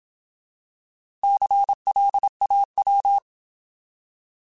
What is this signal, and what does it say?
Morse code, heard as a keyed tone, about 27 wpm, 800 Hz. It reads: CLAW
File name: Alphabetical